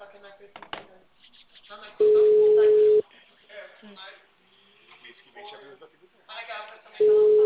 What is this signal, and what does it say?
Ringback tone (Europe, 425 Hz), with speech.
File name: out- -2004- - - .29